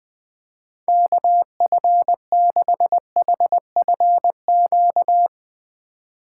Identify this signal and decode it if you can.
Morse code, heard as a keyed tone, about 21 wpm, 700 Hz. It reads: KF6HFQ
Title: USA Call Signs - 6 Characters - 4x Character Spacing - 20wpm